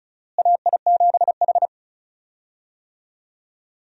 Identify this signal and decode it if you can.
Morse code, heard as a keyed tone, about 36 wpm, 700 Hz. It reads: AI7H
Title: USA Call Signs - 4 Characters - Repeated 2x - 35wpm